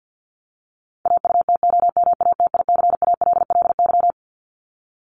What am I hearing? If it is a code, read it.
Morse code, heard as a keyed tone, about 50 wpm, 700 Hz. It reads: AUTOMATICALLY
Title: Top 1751-2000 Words 50wpm